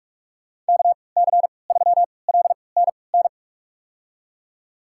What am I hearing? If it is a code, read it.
Morse code, heard as a keyed tone, about 46 wpm, spaced as for 25 wpm, 700 Hz. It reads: KC3LNN
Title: USA Call Signs - 6 Characters - 3x Character Spacing - 45wpm